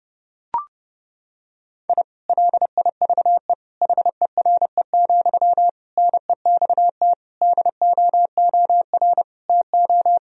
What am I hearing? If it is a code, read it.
Morse code, heard as a keyed tone, about 30 wpm, 700 Hz. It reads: I LIVE HERE, NEXT DOOR TO
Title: Sentences from Top 1000 Words - Review Part 1 - 30wpm